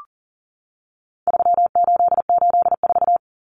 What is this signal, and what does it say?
Morse code, heard as a keyed tone, about 40 wpm, 700 Hz. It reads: 3884
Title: Common Cut Numbers - 4-Digit 40wpm